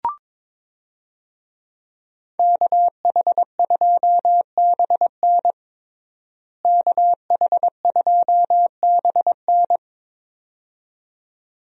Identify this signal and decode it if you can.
Morse code, heard as a keyed tone, about 22 wpm, 700 Hz. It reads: KH2BN KH2BN
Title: USA Call Signs - 5 Characters - Repeated 2x - 22wpm